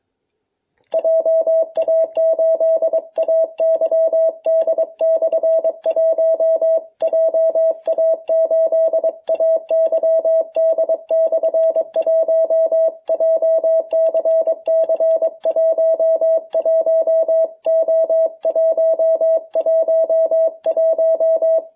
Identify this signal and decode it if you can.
Morse code, heard as a keyed tone, about 23 wpm, 660 Hz. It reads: JA8AYD/1 JA8AYD/1 JCC 11O111